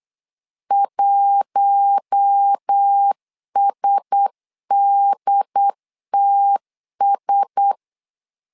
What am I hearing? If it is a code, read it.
Morse code, heard as a keyed tone, about 8 wpm, 800 Hz. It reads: 1SDTS
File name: LEZIONI / 4 Q1S 8wpm 2710